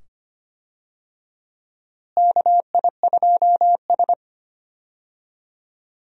Morse code, heard as a keyed tone, about 25 wpm, 700 Hz. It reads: KI2S
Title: USA Call Signs - 4 Characters - Repeated 2x - 25wpm